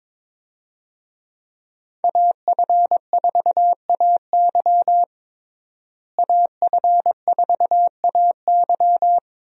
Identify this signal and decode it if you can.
Morse code, heard as a keyed tone, about 22 wpm, 700 Hz. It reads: AF4AY AF4AY
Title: USA Call Signs - 5 Characters - Repeated 3x - 22wpm